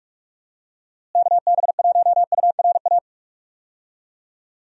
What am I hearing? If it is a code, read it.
Morse code, heard as a keyed tone, about 42 wpm, 700 Hz. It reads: KB1URA
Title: USA Call Signs - 6 Characters - Repeated 3x - 45wpm